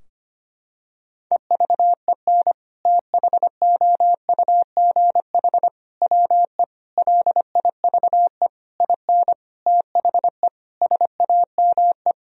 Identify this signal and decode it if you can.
Morse code, heard as a keyed tone, about 25 wpm, 700 Hz. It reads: EVEN THOUGH WE LIVE IN THE SAME
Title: Sentences from Top 1000 Words - Review Part 2 - 25wpm